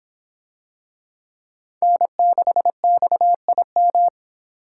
Morse code, heard as a keyed tone, about 25 wpm, 700 Hz. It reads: N6XIM